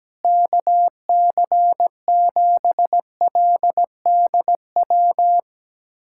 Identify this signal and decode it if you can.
Morse code, heard as a keyed tone, about 17 wpm, 700 Hz. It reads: KC7LDW
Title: USA Callsign DE Callsign - 6 Characters - 17wpm